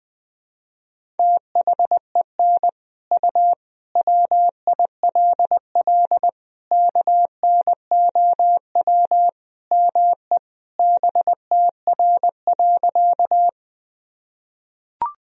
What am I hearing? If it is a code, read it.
Morse code, heard as a keyed tone, about 20 wpm, 700 Hz. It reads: THEN U WILL KNOW ME BTR.
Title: Sentences from Top 400 Words and Abbreviations 20wpm